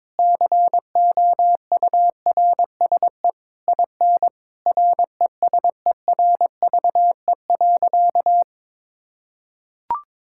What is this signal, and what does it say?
Morse code, heard as a keyed tone, about 22 wpm, 700 Hz. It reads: COURSE IN RESERVE.
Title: Sentences from Top 1000 Words - Review Part 6 - 22wpm